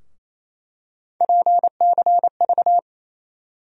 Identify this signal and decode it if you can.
Morse code, heard as a keyed tone, about 28 wpm, 700 Hz. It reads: PCV
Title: USA Call Sign Postfixes - 3 Letters 28wpm